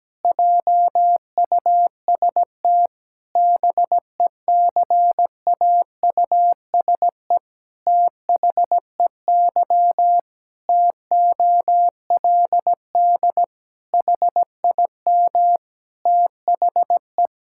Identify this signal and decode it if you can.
Morse code, heard as a keyed tone, about 17 wpm, 700 Hz. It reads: JUST BECAUSE THEY TOLD HIM THE
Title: Pangram Challenge v2 - 17wpm